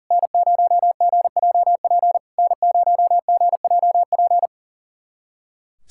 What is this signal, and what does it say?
Morse code, heard as a keyed tone, about 40 wpm, 690 Hz. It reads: N0GJP N0GJP
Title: SOTA QSO Practice 40wpm